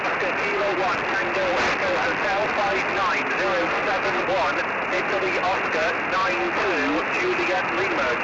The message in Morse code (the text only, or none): none